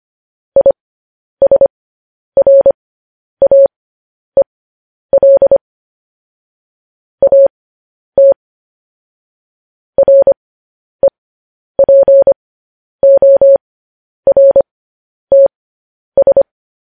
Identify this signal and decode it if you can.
Morse code, heard as a keyed tone, about 25 wpm, spaced as for 10 wpm, 550 Hz. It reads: ISRAEL AT REPORTS